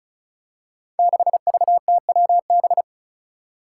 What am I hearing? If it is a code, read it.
Morse code, heard as a keyed tone, about 36 wpm, 700 Hz. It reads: 6VTWB
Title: Five Challenging Letter-Number Combinations 35wpm